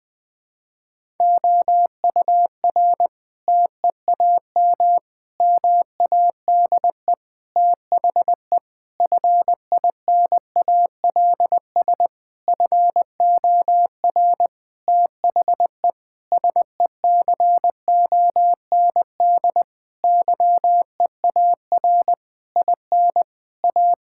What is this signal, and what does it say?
Morse code, heard as a keyed tone, about 21 wpm, 700 Hz. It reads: OUR TEAM MADE THE FINALS FOR THE SECOND YEAR IN A
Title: Sentences with 7-Word Phrases - 20wpm